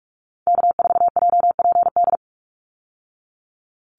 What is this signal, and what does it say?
Morse code, heard as a keyed tone, about 44 wpm, 700 Hz. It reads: K4JPD
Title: Call Signs Repeated 2x - 45wpm